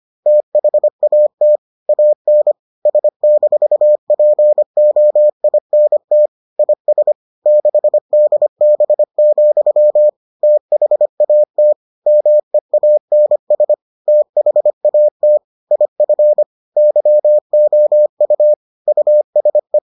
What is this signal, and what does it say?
Morse code, heard as a keyed tone, about 25 wpm, 600 Hz. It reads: THAT AN S-POINT IS 6DB, THAT MEANS THAT IF YOU USE